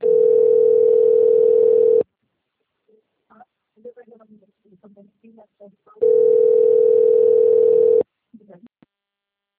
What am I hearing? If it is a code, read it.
Ringback tone (North America), 440 + 480 Hz, with speech.
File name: rg-1600-unknown- - - .92773